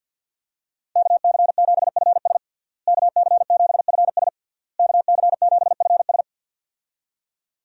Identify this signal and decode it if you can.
Morse code, heard as a keyed tone, about 49 wpm, 700 Hz. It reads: KC7FS KC7FS KC7FS
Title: USA Call Signs - 5 Characters - Repeated 3x - 50wpm